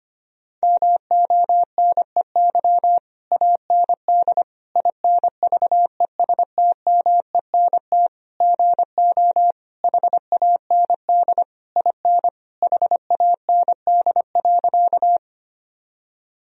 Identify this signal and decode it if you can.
Morse code, heard as a keyed tone, about 25 wpm, 700 Hz. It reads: MONEY AND INVESTMENT GO HAND IN HAND.
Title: Sentences from Top 1000 Words - Review Part 4 - 25wpm